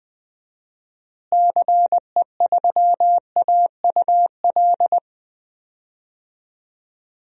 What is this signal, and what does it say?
Morse code, heard as a keyed tone, about 20 wpm, 700 Hz. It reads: CE3AUL
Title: DXCC-100 Call Signs - 6 Characters - Repeated 2x - 20wpm